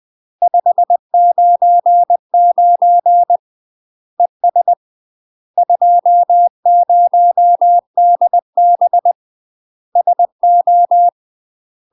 Morse code, heard as a keyed tone, about 20 wpm, 700 Hz. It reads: 599 ES 20DB SO
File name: 20WPM